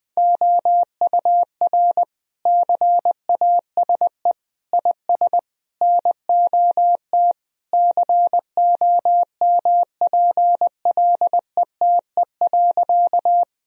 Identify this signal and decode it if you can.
Morse code, heard as a keyed tone, about 20 wpm, 700 Hz. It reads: OUR CASE IS NOT COMPLETE.
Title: Sentences from Top 1000 Words - Review Part 2 - 20wpm